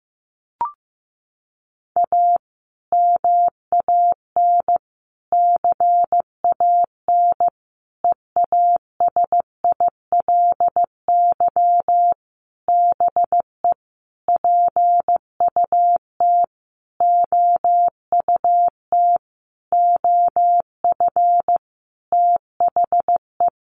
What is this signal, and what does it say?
Morse code, heard as a keyed tone, about 15 wpm, 700 Hz. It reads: A MAN CAN EASILY BE PUT OUT OF THE
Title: Sentences from Top 900 Words 15wpm